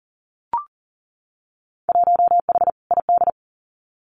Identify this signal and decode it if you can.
Morse code, heard as a keyed tone, about 40 wpm, 700 Hz. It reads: 1H ID
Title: Winter Field Day Exchanges 40wpm